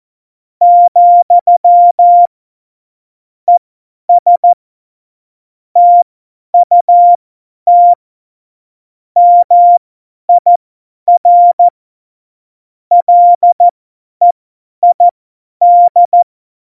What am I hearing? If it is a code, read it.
Morse code, heard as a keyed tone, about 14 wpm, spaced as for 10 wpm, 700 Hz. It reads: , ES TUT MIR LEID
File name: Morsekurs / Witze / Witz 02. April 2019, 50 F70